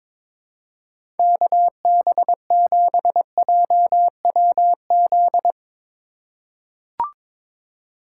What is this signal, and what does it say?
Morse code, heard as a keyed tone, about 23 wpm, 700 Hz. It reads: KB7JWZ
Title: USA Call Signs - 2x Character Spacing - 22wpm